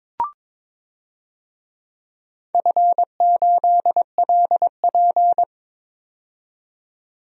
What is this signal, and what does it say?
Morse code, heard as a keyed tone, about 22 wpm, 700 Hz. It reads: F8LP
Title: DXCC-100 Call Signs - 4 Characters - 22wpm